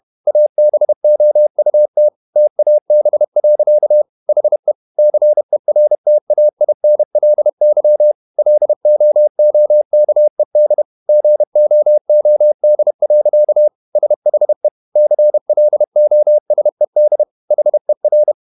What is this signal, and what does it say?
Morse code, heard as a keyed tone, about 31 wpm, 600 Hz. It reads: ABOUT TAB. HE CERTAINLY LOOKED GOOD. SHE CLOSED HER
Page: Never 31wpm | CW med Gnister